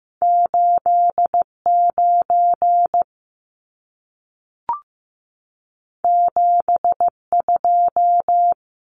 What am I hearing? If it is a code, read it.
Morse code, heard as a keyed tone, about 15 wpm, 700 Hz. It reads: 89 72
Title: Numbers - 2-Digit 15wpm